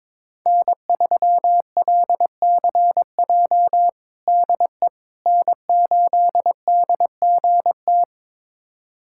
Morse code, heard as a keyed tone, about 22 wpm, 700 Hz. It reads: N3LCJ DE N8DGT
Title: USA Callsign DE Callsign 22wpm